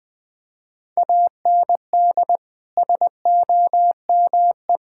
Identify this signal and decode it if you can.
Morse code, heard as a keyed tone, about 20 wpm, 700 Hz. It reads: AND SOME
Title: Sentences with 4-Word Phrases - Spoken Beginning - 20wpm